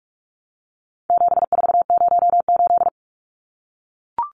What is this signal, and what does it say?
Morse code, heard as a keyed tone, about 44 wpm, 700 Hz. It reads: 7408
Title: Numbers - 4-Digit 45wpm